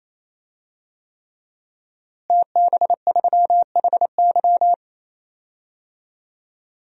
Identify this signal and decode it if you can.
Morse code, heard as a keyed tone, about 28 wpm, 700 Hz. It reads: TB3HY
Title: DXCC-100 Call Signs - 5 Characters - Repeated 3x - 28wpm